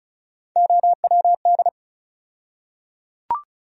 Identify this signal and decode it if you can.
Morse code, heard as a keyed tone, about 34 wpm, 700 Hz. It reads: OWD
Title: USA Call Sign Postfixes - 3 Letters 35wpm